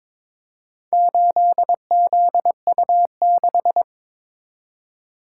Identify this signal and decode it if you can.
Morse code, heard as a keyed tone, about 21 wpm, 700 Hz. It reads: 8ZU6
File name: practice / 22wpm / Four Letter-Number Combinations - 22wpm